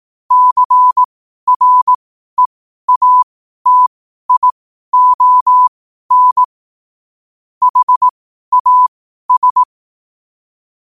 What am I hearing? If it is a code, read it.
Morse code, heard as a keyed tone, about 18 wpm, spaced as for 12 wpm, 1000 Hz. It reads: CREATION HAS
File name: 2026-01-07-12wpm